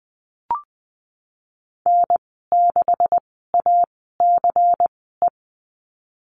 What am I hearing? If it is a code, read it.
Morse code, heard as a keyed tone, about 20 wpm, spaced as for 15 wpm, 700 Hz. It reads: N6ACE
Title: 128 Lesson 21g - Number 6 - Call Signs 20wpm